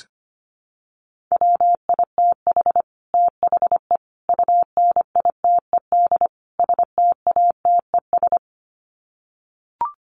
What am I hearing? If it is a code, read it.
Morse code, heard as a keyed tone, about 25 wpm, 700 Hz. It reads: WITH THE UNITED STATES
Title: Sets of 4 Words 25wpm